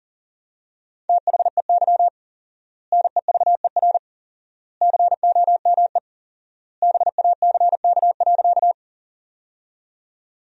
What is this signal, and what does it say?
Morse code, heard as a keyed tone, about 37 wpm, 700 Hz. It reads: THEY NEVER COME BACK.
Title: 4x Word-Spacing - Sentences from Top 300 Words - 40wpm